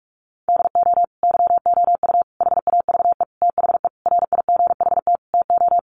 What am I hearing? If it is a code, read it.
Morse code, heard as a keyed tone, about 44 wpm, 700 Hz. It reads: DO YOU HAVE THE RIGHT TO
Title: Sentences from Top 1400 Words 45wpm